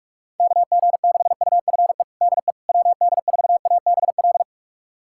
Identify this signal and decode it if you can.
Morse code, heard as a keyed tone, about 46 wpm, 700 Hz. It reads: KG6UFE DE WD4ABL
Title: USA Callsign DE Callsign 45wpm